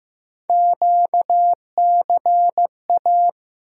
Morse code, heard as a keyed tone, about 15 wpm, 700 Hz. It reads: QCA
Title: USA Call Sign Postfixes - 3 Letters 15wpm